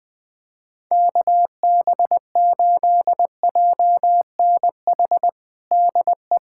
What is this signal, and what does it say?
Morse code, heard as a keyed tone, about 20 wpm, 700 Hz. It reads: KB8JNH DE
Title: USA Callsign DE Callsign - 6 Characters - 20wpm